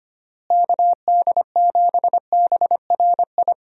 Morse code, heard as a keyed tone, about 25 wpm, 700 Hz. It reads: KD7BRI
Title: USA Call Signs - 6 Characters - Repeated 3x - 25wpm